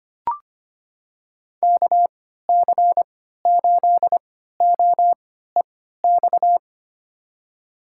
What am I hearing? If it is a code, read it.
Morse code, heard as a keyed tone, about 25 wpm, spaced as for 14 wpm, 690 Hz. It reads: KC8OEX
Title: USA Call Signs - 6 Characters - 3x Character Spacing - 25wpm